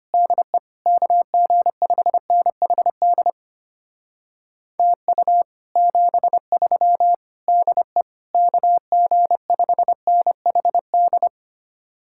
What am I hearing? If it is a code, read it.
Morse code, heard as a keyed tone, about 32 wpm, 700 Hz. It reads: DE KG5NHD TU 73 DE KG5NHD